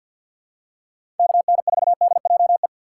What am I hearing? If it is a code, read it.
Morse code, heard as a keyed tone, about 52 wpm, 700 Hz. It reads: KN4DJE
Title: USA Call Signs - 2x Character Spacing - 50wpm